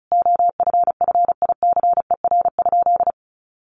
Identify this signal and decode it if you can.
Morse code, heard as a keyed tone, about 34 wpm, 700 Hz. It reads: OFFICER?
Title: Sentences from Top 1200 Words 35wpm